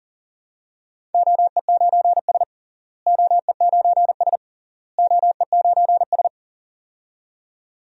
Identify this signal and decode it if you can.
Morse code, heard as a keyed tone, about 40 wpm, 700 Hz. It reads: OE9S OE9S OE9S
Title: DXCC-100 Call Signs - 4 Characters - Repeated 3x - 40wpm